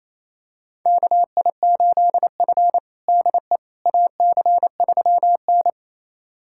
Morse code, heard as a keyed tone, about 28 wpm, 700 Hz. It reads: KI8F DE AC3N
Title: USA Callsign DE Callsign - 4 Characters - 28wpm